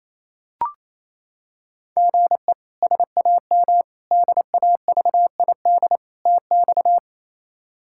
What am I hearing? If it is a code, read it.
Morse code, heard as a keyed tone, about 28 wpm, 700 Hz. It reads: GE SAM DAVID TX
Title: K1USN SST QSO Practice - 28wpm